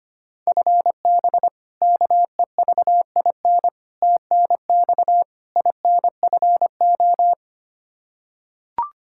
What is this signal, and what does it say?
Morse code, heard as a keyed tone, about 25 wpm, 700 Hz. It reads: FB KEVIN TNX INFO
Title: Practice QSO - Exchange 3 - More Info - 25wpm